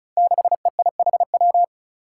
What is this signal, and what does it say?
Morse code, heard as a keyed tone, about 36 wpm, 700 Hz. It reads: 6EIHW
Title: Five Challenging Letter-Number Combinations 35wpm